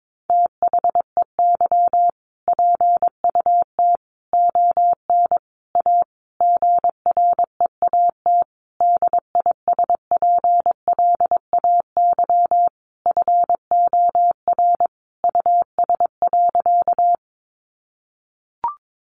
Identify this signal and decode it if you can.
Morse code, heard as a keyed tone, about 22 wpm, 700 Hz. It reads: THEY PUT ON A GREAT DISPLAY FOR US.
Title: Sentences from Top 1200 Words 22wpm